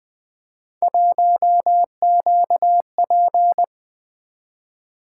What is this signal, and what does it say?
Morse code, heard as a keyed tone, about 24 wpm, 700 Hz. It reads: 1QP
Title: Speed-Racing - Three Letter-Number Combinations - 20wpm